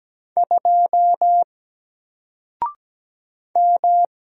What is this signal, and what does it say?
Morse code, heard as a keyed tone, about 17 wpm, 700 Hz. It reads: 2 M
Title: Single Letter-Number Alt 17wpm